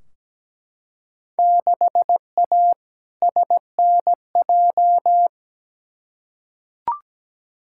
Morse code, heard as a keyed tone, about 17 wpm, 700 Hz. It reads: 6A SNJ
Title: Field Day Exchanges Encore - 17wpm